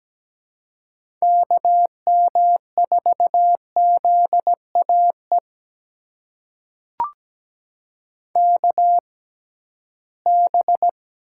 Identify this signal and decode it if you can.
Morse code, heard as a keyed tone, about 17 wpm, spaced as for 13 wpm, 700 Hz. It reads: KM4ZAE KB
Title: USA Call Signs - 6 Characters - 6x Character Spacing - 17wpm